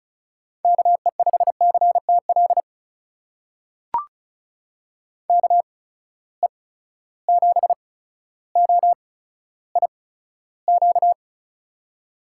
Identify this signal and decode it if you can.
Morse code, heard as a keyed tone, about 36 wpm, spaced as for 15 wpm, 700 Hz. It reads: KE5CTL KE7OIQ
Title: USA Call Signs - 6 Characters - 8x Character Spacing - 35wpm